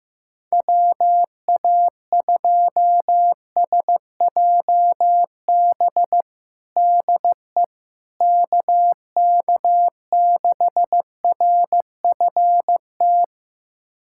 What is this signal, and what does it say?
Morse code, heard as a keyed tone, about 15 wpm, 700 Hz. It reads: WA2SJB DE KK6RFT
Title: USA Callsign DE Callsign - 6 Characters - 15wpm